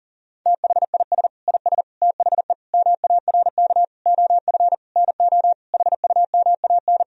Morse code, heard as a keyed tone, about 40 wpm, 700 Hz. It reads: THIS IS THE MARK OF NO HUMAN
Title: Sentences from Top 1000 Words - Review Part 3 - 40wpm